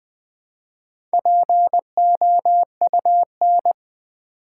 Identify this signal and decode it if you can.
Morse code, heard as a keyed tone, about 20 wpm, 700 Hz. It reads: POUN
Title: Build A Word - 1 Letter At A Time - 5-Letter Words - 20wpm